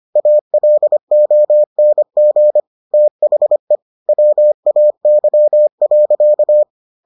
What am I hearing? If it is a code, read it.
Morse code, heard as a keyed tone, about 25 wpm, 600 Hz. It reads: ALONG THE WAY.